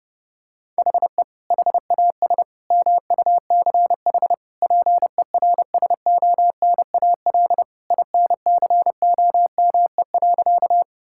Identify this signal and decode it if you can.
Morse code, heard as a keyed tone, about 30 wpm, 700 Hz. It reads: HE HAS MUCH PERSONAL INCOME.
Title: Sentences from Top 1000 Words - Review Part 5 - 30wpm